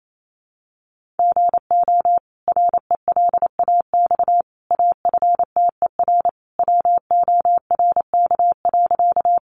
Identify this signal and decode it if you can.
Morse code, heard as a keyed tone, about 28 wpm, 700 Hz. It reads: GO RELAX AFTER WORK.
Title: Sentences from Top 1400 Words 28wpm